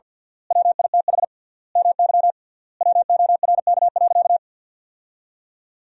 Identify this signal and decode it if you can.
Morse code, heard as a keyed tone, about 53 wpm, 700 Hz. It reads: WITH MY WORK.
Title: 3x Word-Spacing - Sentences from Top 300 Words - 50wpm